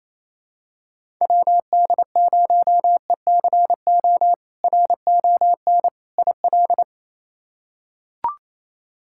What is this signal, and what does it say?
Morse code, heard as a keyed tone, about 28 wpm, 700 Hz. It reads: WD0ECO RON IL
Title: K1USN SST QSO Practice - 28wpm